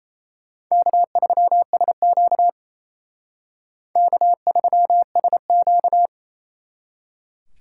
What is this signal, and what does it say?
Morse code, heard as a keyed tone, about 35 wpm, 700 Hz. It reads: K3SQ K3SQ
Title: Speed Racing - USA 1x2 Call Signs 28wpm